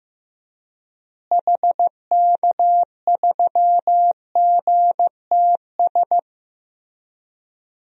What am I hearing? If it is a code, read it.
Morse code, heard as a keyed tone, about 15 wpm, 700 Hz. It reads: HK3GTS
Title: DXCC-100 Call Signs - 6 Characters - Repeated 3x - 15wpm